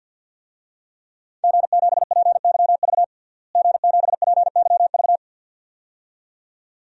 Morse code, heard as a keyed tone, about 49 wpm, 700 Hz. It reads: G7PYV G7PYV
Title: DXCC-100 Call Signs - 5 Characters - Repeated 2x - 50wpm